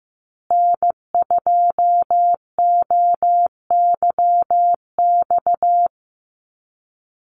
Morse code, heard as a keyed tone, about 15 wpm, 700 Hz. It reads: N2OYX
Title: USA Call Signs Repeated 3x 15wpm